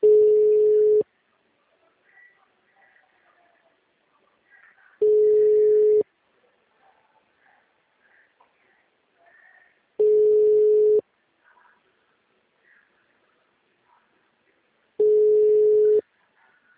Ringback tone (Europe, 425 Hz).